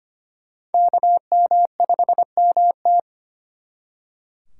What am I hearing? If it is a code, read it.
Morse code, heard as a keyed tone, about 34 wpm, spaced as for 30 wpm, 700 Hz. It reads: KM5MT
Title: Speed Racing - USA Call Signs 25wpm